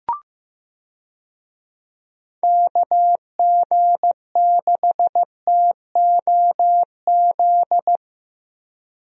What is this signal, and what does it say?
Morse code, heard as a keyed tone, about 15 wpm, 700 Hz. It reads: KG6TOZ
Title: USA Call Signs - 6 Characters - Repeated 2x - 15wpm